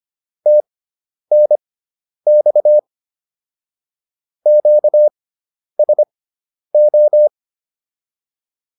Morse code, heard as a keyed tone, about 25 wpm, spaced as for 10 wpm, 600 Hz. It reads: TNX QSO
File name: QSO207 10